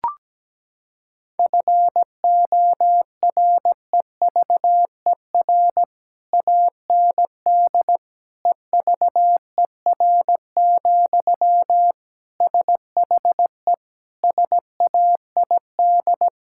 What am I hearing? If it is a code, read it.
Morse code, heard as a keyed tone, about 17 wpm, 700 Hz. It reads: FOREVER AND EVER, SHE SAID
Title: Sentences from Top 1600 Words 17wpm